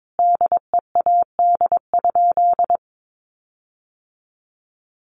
Morse code, heard as a keyed tone, about 22 wpm, 700 Hz. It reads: DEAD?
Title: Sentences from Top 1500 Words 22wpm